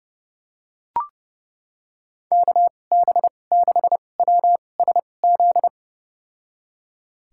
Morse code, heard as a keyed tone, about 30 wpm, spaced as for 22 wpm, 700 Hz. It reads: KB6WSZ
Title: USA Call Signs - 6 Characters - 2x Character Spacing - 30wpm